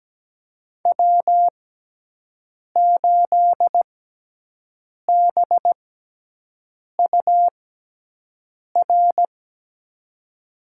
Morse code, heard as a keyed tone, about 17 wpm, spaced as for 6 wpm, 700 Hz. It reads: W8BUR
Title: USA Call Signs - 6x Character Spacing - 17wpm